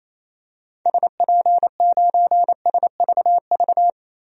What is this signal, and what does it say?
Morse code, heard as a keyed tone, about 28 wpm, 700 Hz. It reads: SP9SVV
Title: DXCC-100 Call Signs - 6 Characters - 28wpm